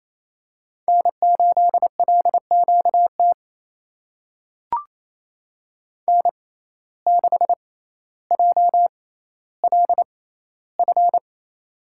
Morse code, heard as a keyed tone, about 27 wpm, spaced as for 14 wpm, 700 Hz. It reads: N8LQT N6JLF
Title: USA Call Signs - 5 Characters - 6x Character Spacing - 28wpm